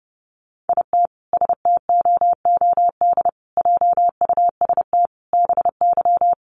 Morse code, heard as a keyed tone, about 30 wpm, 700 Hz. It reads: IT STOOD JUST BY
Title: Sentences from Top 1400 Words 30wpm